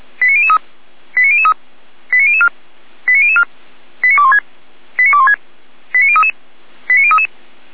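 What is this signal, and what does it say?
Five-tone selective call (ZVEI), heard as quick runs of five tones, tones of about 70 ms. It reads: 89003 89003 89004 89004 89216 89216 89033 89033
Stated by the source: ZVEI five-tone selective call received on an SDR.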